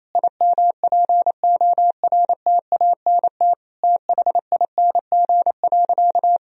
Morse code, heard as a keyed tone, about 28 wpm, 700 Hz. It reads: IMPORTANT THING.
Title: Sentences from Top 1000 Words - Review Part 1 - 28wpm